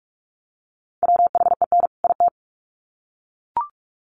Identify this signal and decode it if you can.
Morse code, heard as a keyed tone, about 44 wpm, 700 Hz. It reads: WHEN IT
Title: Sets of 2 Words 45wpm